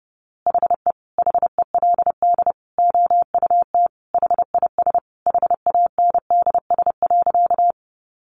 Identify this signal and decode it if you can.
Morse code, heard as a keyed tone, about 30 wpm, 700 Hz. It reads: HE HELD OUT HIS HANDS.
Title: A Sound Of Thunder 30wpm